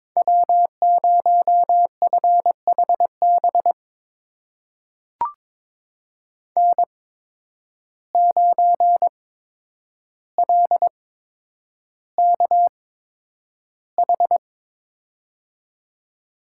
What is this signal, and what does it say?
Morse code, heard as a keyed tone, about 22 wpm, spaced as for 9 wpm, 700 Hz. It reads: W0FHB N9LKH
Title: USA Call Signs - 5 Characters - 8x Character Spacing - 22wpm